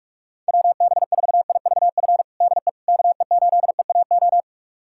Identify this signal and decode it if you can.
Morse code, heard as a keyed tone, about 47 wpm, 700 Hz. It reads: WB4IVF DE KE8EAO
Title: USA Callsign DE Callsign 45wpm